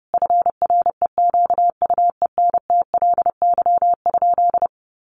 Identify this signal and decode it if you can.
Morse code, heard as a keyed tone, about 30 wpm, 700 Hz. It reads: FREQUENTLY?
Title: Sentences from Top 1300 Words 30wpm